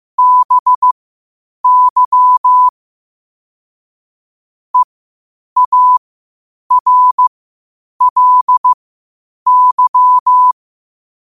Morse code, heard as a keyed tone, about 15 wpm, spaced as for 8 wpm, 1000 Hz. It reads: BY EARLY
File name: qotdfiles / 8wpm